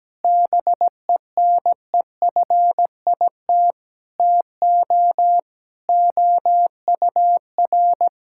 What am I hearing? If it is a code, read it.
Morse code, heard as a keyed tone, about 17 wpm, 700 Hz. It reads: BENEFIT TO OUR
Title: Sentences from Top 1000 Words - Review Part 6 - 17wpm